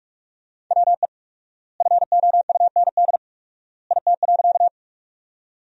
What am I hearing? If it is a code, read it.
Morse code, heard as a keyed tone, about 47 wpm, 700 Hz. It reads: WE FOUND IT.
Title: 4x Word-Spacing - Sentences from Top 300 Words - 45wpm